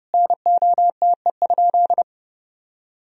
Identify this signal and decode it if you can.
Morse code, heard as a keyed tone, about 32 wpm, 700 Hz. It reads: NOTE?
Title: Sentences from Top 1000 Words - Review Part 2 - 30wpm